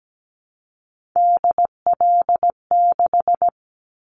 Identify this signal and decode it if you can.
Morse code, heard as a keyed tone, about 17 wpm, 700 Hz. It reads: DL6
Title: Callsign Prefixes for Top 25 DXCC - Repeated 3x - 17wpm